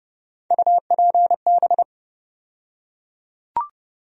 Morse code, heard as a keyed tone, about 30 wpm, 700 Hz. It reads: UPB
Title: USA Call Sign Postfixes - 3 Letters 30wpm